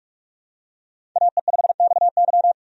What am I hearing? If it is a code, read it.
Morse code, heard as a keyed tone, about 43 wpm, 700 Hz. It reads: AE5XY
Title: USA Call Signs - 5 Characters - 45wpm